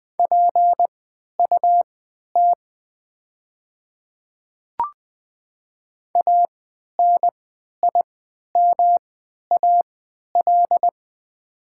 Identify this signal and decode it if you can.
Morse code, heard as a keyed tone, about 20 wpm, spaced as for 11 wpm, 700 Hz. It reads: PUT ANIMAL
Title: 3x Character Spacing - Top 200 Words - 20wpm